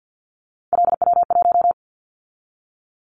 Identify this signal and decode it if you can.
Morse code, heard as a keyed tone, about 50 wpm, 700 Hz. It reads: LW1
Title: Callsign Prefixes for Top 100 DXCC - 50wpm